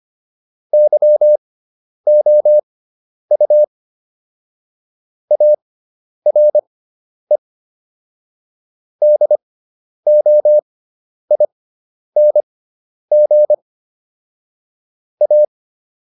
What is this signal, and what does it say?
Morse code, heard as a keyed tone, about 25 wpm, spaced as for 10 wpm, 600 Hz. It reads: YOU ARE DOING A